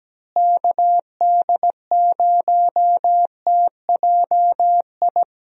Morse code, heard as a keyed tone, about 17 wpm, 700 Hz. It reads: KD0TJI
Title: USA Call Signs - 6 Characters - Repeated 3x - 17wpm